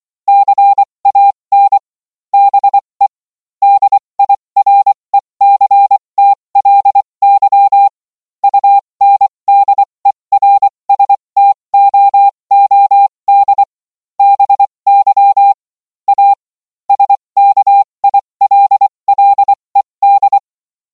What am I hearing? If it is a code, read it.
Morse code, heard as a keyed tone, about 24 wpm, spaced as for 20 wpm, 800 Hz. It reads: CAN BE DIRECTLY UNDERSTOOD BY A SKILLED